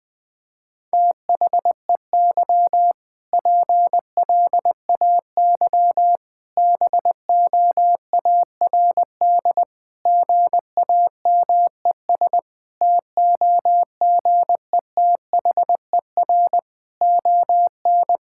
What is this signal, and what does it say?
Morse code, heard as a keyed tone, about 19 wpm, 700 Hz. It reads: THEY PLAY BOARD GAMES TOGETHER ON
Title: Difficult Sentences with Hints - 20wpm